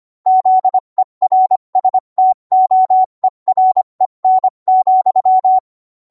Morse code, heard as a keyed tone, about 25 wpm, 750 Hz. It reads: ZERSTOEREN,